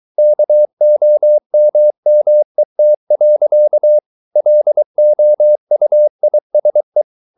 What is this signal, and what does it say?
Morse code, heard as a keyed tone, about 23 wpm, 600 Hz. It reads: KOMMET. LOUISE
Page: Ferie 23 wpm